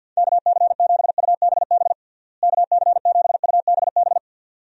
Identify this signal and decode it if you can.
Morse code, heard as a keyed tone, about 49 wpm, 700 Hz. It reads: KC7UBB KC7UBB
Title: USA Call Signs - 6 Characters - Repeated 2x - 50wpm